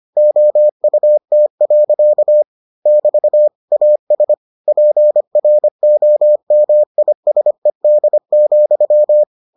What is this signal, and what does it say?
Morse code, heard as a keyed tone, about 25 wpm, 600 Hz. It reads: OUT. <BT> AS PROMISED,